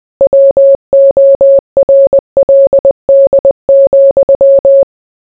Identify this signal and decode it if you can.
Morse code, heard as a keyed tone, about 20 wpm, 550 Hz. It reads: WORLD,